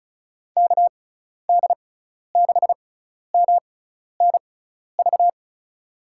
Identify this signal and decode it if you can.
Morse code, heard as a keyed tone, about 34 wpm, spaced as for 12 wpm, 700 Hz. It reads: KD6MNV
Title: USA Call Signs - 6 Characters - 6x Character Spacing - 35wpm